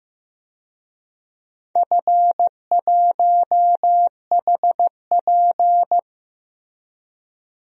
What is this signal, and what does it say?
Morse code, heard as a keyed tone, about 15 wpm, 700 Hz. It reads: F1HP
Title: DXCC-100 Call Signs - 4 Characters - 15wpm